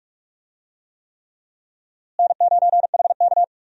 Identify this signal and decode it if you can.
Morse code, heard as a keyed tone, about 42 wpm, 700 Hz. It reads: N9HK
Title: USA Call Signs - 4 Characters - Repeated 3x - 45wpm